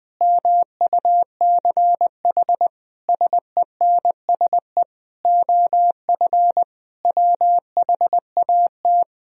Morse code, heard as a keyed tone, about 20 wpm, 700 Hz. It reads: MUCH SENSE OF WHAT
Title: Sentences from Top 1000 Words - Review Part 4 - 20wpm